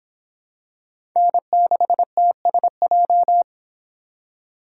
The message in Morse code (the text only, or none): N6TSJ